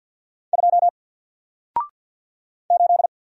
Morse code, heard as a keyed tone, about 50 wpm, 700 Hz. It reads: 2 8
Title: Single Character (Letter:Number:Punctuation:Prosign) 50wpm